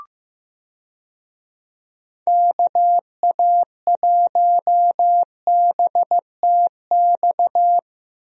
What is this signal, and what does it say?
Morse code, heard as a keyed tone, about 15 wpm, 700 Hz. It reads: KA1BTX
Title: USA Call Signs - 6 Characters - Repeated 2x - 15wpm